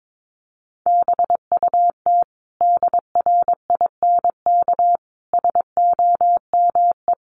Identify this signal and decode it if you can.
Morse code, heard as a keyed tone, about 22 wpm, 700 Hz. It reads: BUT DRINK SOME
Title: Sentences from Top 1500 Words 22wpm